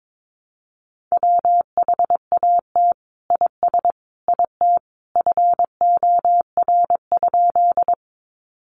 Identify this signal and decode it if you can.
Morse code, heard as a keyed tone, about 22 wpm, 700 Hz. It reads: WHAT IS IT FOR?
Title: Sentences from Top 100 Words 22wpm